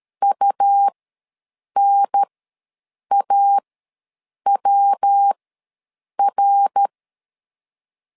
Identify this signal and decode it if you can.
Morse code, heard as a keyed tone, about 13 wpm, spaced as for 7 wpm, 800 Hz. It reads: UNAWR
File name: LEZIONI / 3 MWRO / 3 MWR ws 12wpm 1809